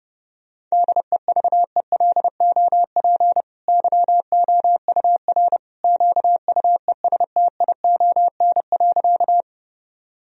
Morse code, heard as a keyed tone, about 30 wpm, 700 Hz. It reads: DEVELOP YOUR QUESTION.
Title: Sentences from Top 1000 Words - Review Part 2 - 30wpm